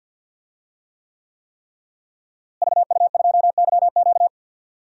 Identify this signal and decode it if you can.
Morse code, heard as a keyed tone, about 49 wpm, 700 Hz. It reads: VU2YQ
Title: DXCC-100 Call Signs - 5 Characters - Repeated 3x - 50wpm